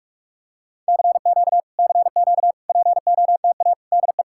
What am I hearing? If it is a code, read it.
Morse code, heard as a keyed tone, about 49 wpm, 700 Hz. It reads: CQ CQ POTA DE